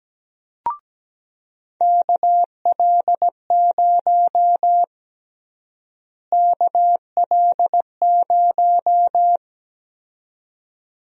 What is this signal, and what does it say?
Morse code, heard as a keyed tone, about 17 wpm, 700 Hz. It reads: KL0 KL0
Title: USA Call Sign Prefixes - Repeated 2x - 17wpm